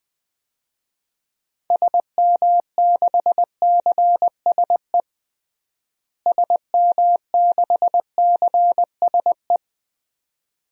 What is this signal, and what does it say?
Morse code, heard as a keyed tone, about 20 wpm, 700 Hz. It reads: SM6CSE SM6CSE
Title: DXCC-100 Call Signs - 6 Characters - Repeated 2x - 20wpm